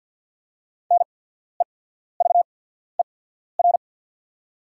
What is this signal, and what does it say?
Morse code, heard as a keyed tone, about 48 wpm, spaced as for 14 wpm, 690 Hz. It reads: NEVER
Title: 8x Character Spacing - Top 300 Words - 50wpm